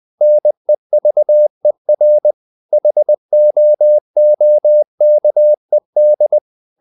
Morse code, heard as a keyed tone, about 20 wpm, 600 Hz. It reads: NEVER HOOKED
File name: the old man and the sea part 2 20wpm